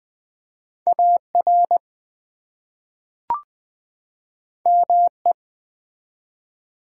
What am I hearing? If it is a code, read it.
Morse code, heard as a keyed tone, about 20 wpm, 700 Hz. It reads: AR ME
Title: ARRL Sections v2 - 20wpm